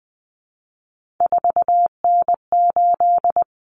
Morse code, heard as a keyed tone, about 20 wpm, 700 Hz. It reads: 4N8
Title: Common Cut Numbers - 3-Digit 20wpm